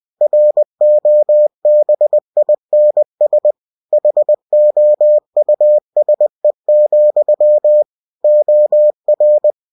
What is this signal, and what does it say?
Morse code, heard as a keyed tone, about 20 wpm, 600 Hz. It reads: ROBINS HOUSE, OR
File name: cw-books / winnie-the-pooh / winnie-20wpm-page-0090